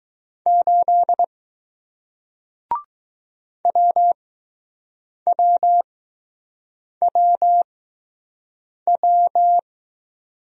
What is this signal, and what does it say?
Morse code, heard as a keyed tone, about 20 wpm, spaced as for 6 wpm, 700 Hz. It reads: 8 WWWW